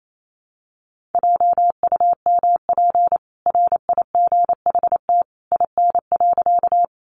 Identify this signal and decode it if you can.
Morse code, heard as a keyed tone, about 28 wpm, 700 Hz. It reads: JUMP RIGHT IN.
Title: Sentences from Top 1300 Words 28wpm